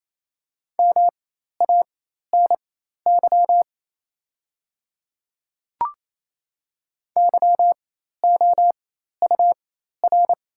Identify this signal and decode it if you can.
Morse code, heard as a keyed tone, about 28 wpm, spaced as for 13 wpm, 690 Hz. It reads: MANY YOUR
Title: 4x Character Spacing - Top 100 Words - 28wpm